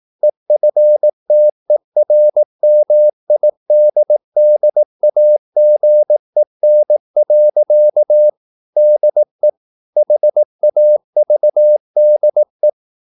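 Morse code, heard as a keyed tone, about 18 wpm, 600 Hz. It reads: EFTERMIDDAGEN. DE HAVDE
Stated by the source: Slow (QRS) Morse practice at 18 wpm, CW med Gnister.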